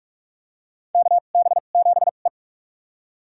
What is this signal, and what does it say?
Morse code, heard as a keyed tone, about 44 wpm, spaced as for 32 wpm, 700 Hz. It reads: KB7E
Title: USA Call Signs - 2x Character Spacing - 45wpm